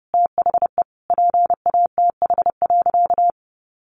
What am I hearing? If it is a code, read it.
Morse code, heard as a keyed tone, about 30 wpm, 700 Hz. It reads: THE PATH.
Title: A Sound Of Thunder 30wpm